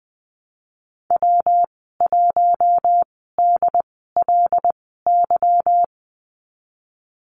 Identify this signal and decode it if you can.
Morse code, heard as a keyed tone, about 20 wpm, spaced as for 15 wpm, 700 Hz. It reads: W1DLY
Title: 154 Lesson 25g - Letter p - Call Signs 20wpm